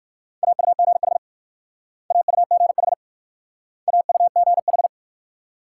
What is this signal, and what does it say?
Morse code, heard as a keyed tone, about 57 wpm, 690 Hz. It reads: AUGH AUGH AUGH